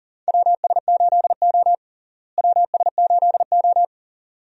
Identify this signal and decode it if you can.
Morse code, heard as a keyed tone, about 40 wpm, 700 Hz. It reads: WS8O WS8O
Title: USA Call Signs - 4 Characters - Repeated 2x - 40wpm